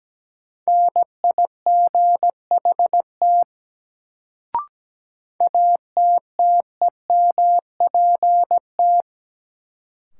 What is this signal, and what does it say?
Morse code, heard as a keyed tone, about 17 wpm, 700 Hz. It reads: NIGHT ATTEMPT
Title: Sets of 4 Words - New Word Drill - 17wpm